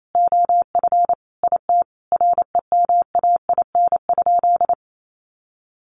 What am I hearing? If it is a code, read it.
Morse code, heard as a keyed tone, about 28 wpm, 700 Hz. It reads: OF IT REMAIN?
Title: Sentences from Top 1200 Words 28wpm